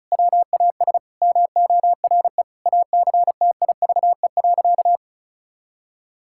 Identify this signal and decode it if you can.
Morse code, heard as a keyed tone, about 34 wpm, 700 Hz. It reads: WAS MORE ACTIVE.
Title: Sentences from Top 1000 Words - Review Part 6 - 35wpm